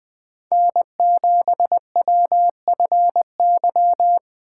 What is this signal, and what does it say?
Morse code, heard as a keyed tone, about 20 wpm, 700 Hz. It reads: N7WFY
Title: Speed Racing - USA 1x3 Call Signs 17wpm